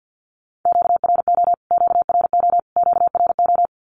Morse code, heard as a keyed tone, about 49 wpm, 700 Hz. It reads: QRO QRO QRO
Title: Q-Codes Repeated 3x 50wpm